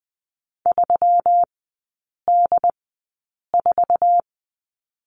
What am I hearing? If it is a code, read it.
Morse code, heard as a keyed tone, about 20 wpm, spaced as for 8 wpm, 700 Hz. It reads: 3D4
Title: 133 Lesson 22e - Question Mark - Three Characters 20wpm